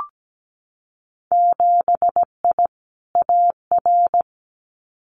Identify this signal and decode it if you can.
Morse code, heard as a keyed tone, about 17 wpm, 700 Hz. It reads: 7I AR
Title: Winter Field Day Exchanges Encore 17wpm